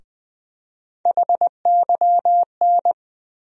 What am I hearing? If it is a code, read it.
Morse code, heard as a keyed tone, about 20 wpm, 700 Hz. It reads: HYN
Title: USA Call Sign Postfixes - 3 Letters 20wpm